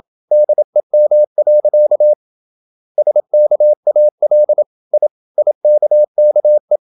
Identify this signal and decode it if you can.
Morse code, heard as a keyed tone, about 27 wpm, 600 Hz. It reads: DEM. SKAL I IKKE
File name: Ferie QRQ 27wpm